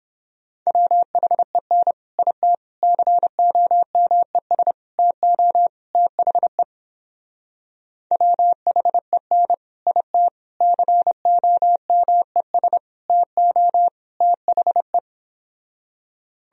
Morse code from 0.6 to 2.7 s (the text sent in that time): WHEN IT